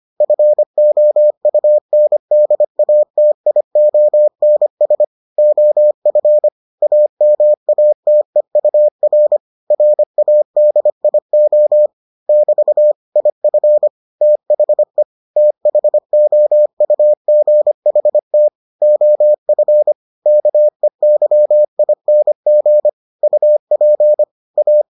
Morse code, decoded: FOUNDATIONS OF AMATEUR RADIO <BT> IF THE THOUGHT OF KEYING UP A

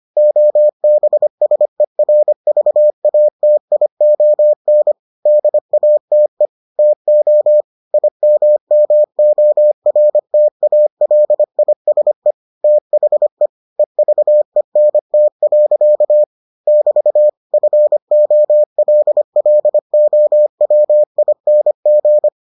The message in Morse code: OBSERVATION DATE TO IMMORTALISE THE EVENT. <BT> FOLLOWING

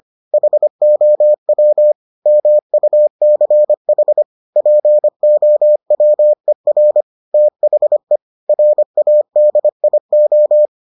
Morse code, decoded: HOW MUCH POWER THE RADIO

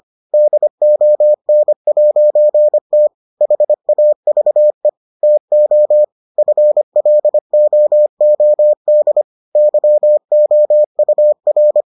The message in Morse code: DON'T HAVE TO FLOOD YOUR

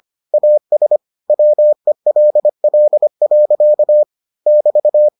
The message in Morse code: AS WELL. <BT>

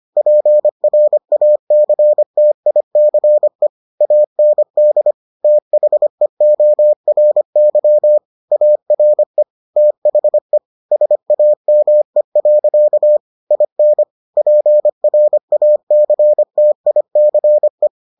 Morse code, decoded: PRACTICE AND THEORY ARE THE SAME. IN PRACTICE

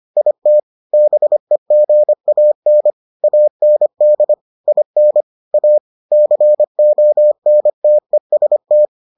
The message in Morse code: IT BEGAN AND IN A CONTEST

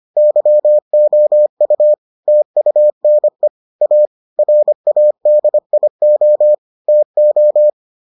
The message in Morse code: YOU TUNE A RADIO TO

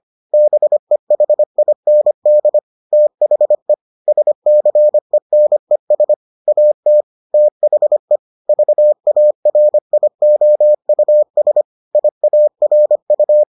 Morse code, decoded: BEHIND THE SCENES AT THE VARIOUS IARU

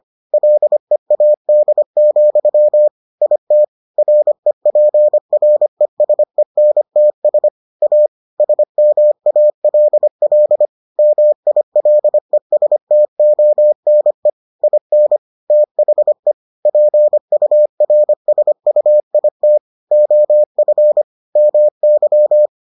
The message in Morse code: LEAD, IT REPRESENTS A SMALL MILESTONE IN THE PURSUIT OF MY

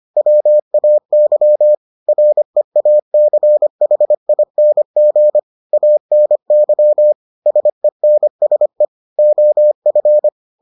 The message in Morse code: WAY REACHING ANY SENSE OF

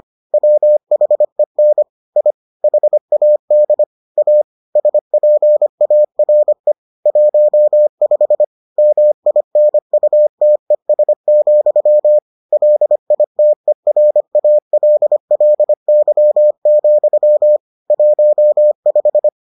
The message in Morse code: WHEN I HAD A SPARE 15 MINUTES, LITERALLY, 15

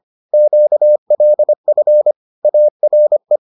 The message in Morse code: QLF ARE

